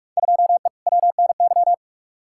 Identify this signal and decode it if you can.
Morse code, heard as a keyed tone, about 45 wpm, 700 Hz. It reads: 2E WNY
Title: Field Day Exchanges Encore - 45wpm